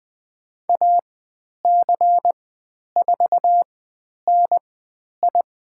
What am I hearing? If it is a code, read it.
Morse code, heard as a keyed tone, about 20 wpm, spaced as for 10 wpm, 700 Hz. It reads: AC4NI
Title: 095 Lesson 16g - Letter c - Call Signs 20wpm